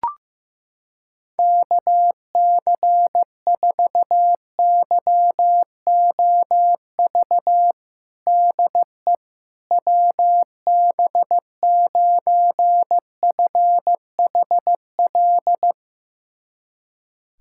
Morse code, heard as a keyed tone, about 15 wpm, 700 Hz. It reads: KC4YOV DE WB9FHL
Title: USA Callsign DE Callsign - 6 Characters - 15wpm